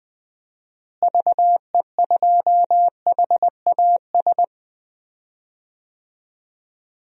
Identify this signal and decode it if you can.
Morse code, heard as a keyed tone, about 20 wpm, 700 Hz. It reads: VE2HAS
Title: DXCC-100 Call Signs - 6 Characters - Repeated 3x - 20wpm